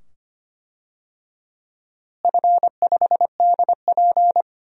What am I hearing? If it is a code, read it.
Morse code, heard as a keyed tone, about 25 wpm, 700 Hz. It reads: F5DP
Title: DXCC-100 Call Signs - 4 Characters - Repeated 2x - 25wpm